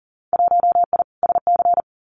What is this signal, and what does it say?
Morse code, heard as a keyed tone, about 40 wpm, 700 Hz. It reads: 1I SC
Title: Winter Field Day Exchanges 40wpm